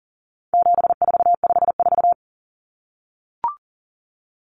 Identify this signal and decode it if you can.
Morse code, heard as a keyed tone, about 40 wpm, 700 Hz. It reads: 7454
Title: Numbers - 4-Digit 40wpm